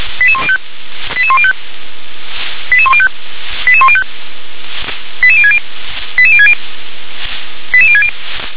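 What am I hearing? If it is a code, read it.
Five-tone selective call (ZVEI), heard as quick runs of five tones, tones of about 70 ms. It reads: 88105 88105 88105 88105 88060 88060 88060
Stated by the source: ZVEI five-tone selective call received by SDR.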